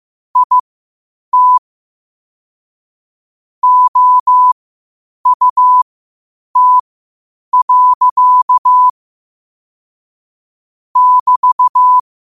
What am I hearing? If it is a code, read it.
Morse code, heard as a keyed tone, about 15 wpm, spaced as for 8 wpm, 1000 Hz. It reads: IT OUT. <BT>
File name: qotdfiles / 8wpm